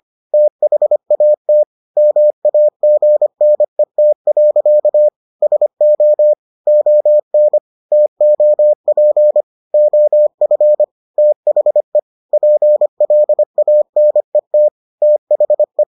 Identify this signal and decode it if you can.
Morse code, heard as a keyed tone, about 25 wpm, 600 Hz. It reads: THAT MAGNET. SO ON TOP OF THE PLANET THE